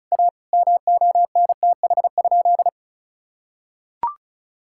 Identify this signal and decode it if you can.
Morse code, heard as a keyed tone, about 36 wpm, 700 Hz. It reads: A MONTH?
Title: Sentences from Top 1000 Words - Review Part 5 - 35wpm